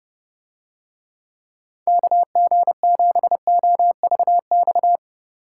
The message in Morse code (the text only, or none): KG7OVX